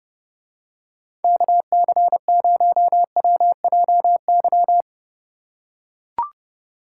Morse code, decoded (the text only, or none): KC0WJY